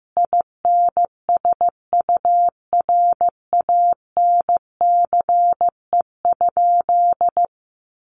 Morse code, heard as a keyed tone, about 15 wpm, 700 Hz. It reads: INSURANCE?
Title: Sentences from Top 1100 Words 15wpm